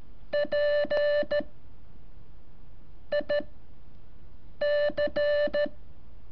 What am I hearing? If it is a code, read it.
Morse code, heard as a keyed tone, about 13 wpm, 580 Hz. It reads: P I C
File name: curso / ejercicio 18